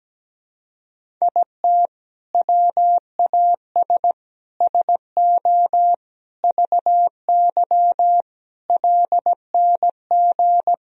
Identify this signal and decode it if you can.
Morse code, heard as a keyed tone, about 17 wpm, 700 Hz. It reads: IT WAS SO VY LNG
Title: Sentences from Top 500 Words and Abbreviations 17wpm